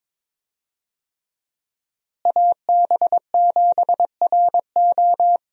Morse code, heard as a keyed tone, about 22 wpm, 700 Hz. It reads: AB7RO
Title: USA Call Signs - 5 Characters - Repeated 3x - 22wpm